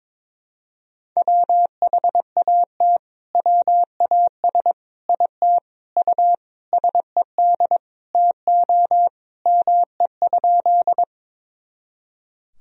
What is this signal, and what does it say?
Morse code, heard as a keyed tone, about 21 wpm, 700 Hz. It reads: WHAT WAS IT U SED TO ME?
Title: Sentences from Top 200 Words and Abbreviations 22wpm